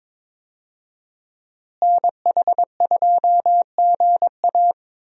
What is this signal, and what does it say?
Morse code, heard as a keyed tone, about 22 wpm, 700 Hz. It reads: NH2GA
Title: USA Call Signs - 5 Characters - Repeated 3x - 22wpm